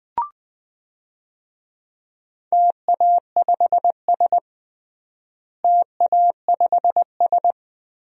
Morse code, heard as a keyed tone, about 20 wpm, 700 Hz. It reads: TA5S TA5S
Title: DXCC-100 Call Signs - 4 Characters - Repeated 2x - 20wpm